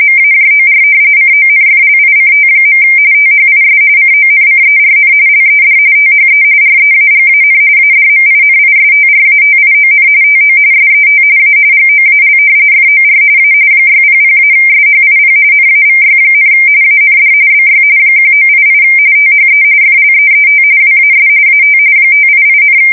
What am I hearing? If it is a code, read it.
RTTY, about 45.45 baud. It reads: MAJOR CONFLICT. THIS CONCERN LED THE REAGAN ADMINISTRATION TO SET A GOAL FOR A 600-SHIP NAVY, AND BY 1988 THE FLEET WAS AT 588,
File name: fsk170 / usnavy5